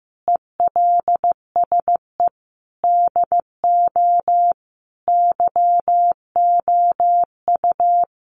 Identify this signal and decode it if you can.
Morse code, heard as a keyed tone, about 15 wpm, 700 Hz. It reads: ELSE DO YOU
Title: Sentences from Top 800 Words 15wpm